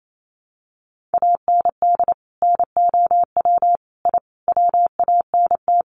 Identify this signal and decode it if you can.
Morse code, heard as a keyed tone, about 28 wpm, 700 Hz. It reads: AND NOW I WANT
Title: A Sound Of Thunder 28wpm